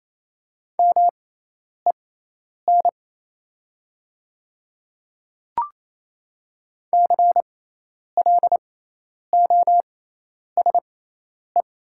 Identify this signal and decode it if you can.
Morse code, heard as a keyed tone, about 28 wpm, spaced as for 10 wpm, 700 Hz. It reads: MEN CLOSE
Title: 6x Character Spacing - Top 300 Words - 28wpm